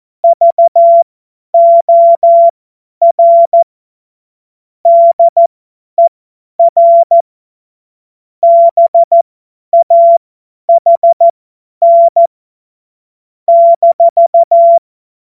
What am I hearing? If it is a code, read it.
Morse code, heard as a keyed tone, about 14 wpm, spaced as for 10 wpm, 680 Hz. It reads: VOR DER BAHN -
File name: Morsekurs / Witze / Witz 20 Februar 2024 50 F70